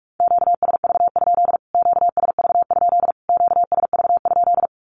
Morse code, heard as a keyed tone, about 44 wpm, 700 Hz. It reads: QSV? QSV? QSV?
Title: Q-Codes Repeated 3x 45wpm